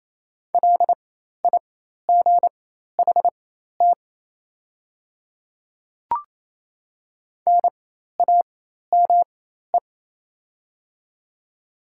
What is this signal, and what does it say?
Morse code, heard as a keyed tone, about 28 wpm, spaced as for 13 wpm, 690 Hz. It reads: LIGHT NAME
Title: 4x Character Spacing - Top 200 Words - 28wpm